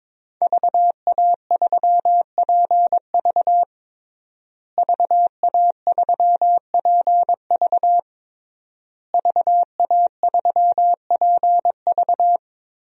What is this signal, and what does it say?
Morse code, heard as a keyed tone, about 21 wpm, 700 Hz. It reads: VA3PV VA3PV VA3PV
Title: DXCC-100 Call Signs - 5 Characters - Repeated 3x - 22wpm